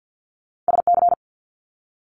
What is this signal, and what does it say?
Morse code, heard as a keyed tone, about 50 wpm, 700 Hz. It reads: SC
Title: Multi-letter Phonemes Quick Fire 50wpm